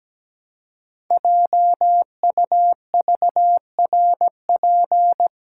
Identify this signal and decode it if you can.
Morse code, heard as a keyed tone, about 17 wpm, 700 Hz. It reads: JUVRP
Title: Code Groups - 5 Letters - 17wpm